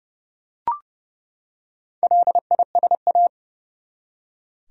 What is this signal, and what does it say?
Morse code, heard as a keyed tone, about 30 wpm, 700 Hz. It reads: LISA
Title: USA Common Names - Max 4 Letters - 30wpm